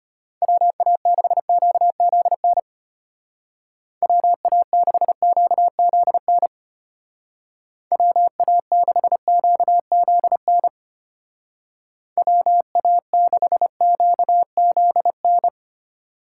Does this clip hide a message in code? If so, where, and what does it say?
Yes: Morse code from 0.4 to 15.5 s, heard as a keyed tone, about 33 wpm, 700 Hz. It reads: WA6QZN WA6QZN WA6QZN WA6QZN